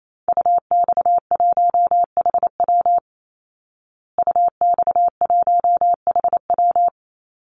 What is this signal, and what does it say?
Morse code, heard as a keyed tone, about 28 wpm, 700 Hz. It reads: UX1HW UX1HW
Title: Call Signs Repeated 2x - 28wpm